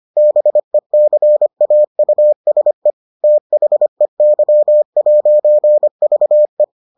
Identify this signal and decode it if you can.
Morse code, heard as a keyed tone, about 25 wpm, 600 Hz. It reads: BECAUSE THEY'VE